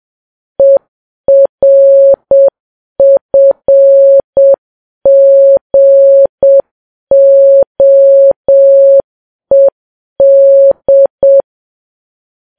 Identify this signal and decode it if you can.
Morse code, heard as a keyed tone, about 7 wpm, 550 Hz. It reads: ERFGOED
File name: Files Web tls / Morse on9fv / morse7kps